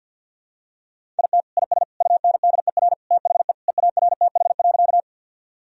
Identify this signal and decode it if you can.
Morse code, heard as a keyed tone, about 50 wpm, 700 Hz. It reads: IT IS UNDER THE EARTH.
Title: Sentences from Top 1000 Words - Review Part 1 - 50wpm